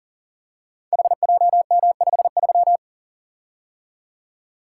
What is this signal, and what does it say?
Morse code, heard as a keyed tone, about 41 wpm, 690 Hz. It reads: HJM53
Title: Code Groups - 5 Letter:Numbers - 40wpm